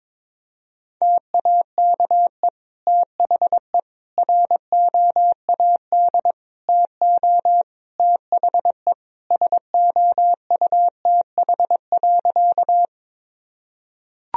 Morse code, heard as a keyed tone, about 22 wpm, 700 Hz. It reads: TAKE THE ROAD TO THE SOUTH.
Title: Sentences from Top 1000 Words - Review Part 2 - 22wpm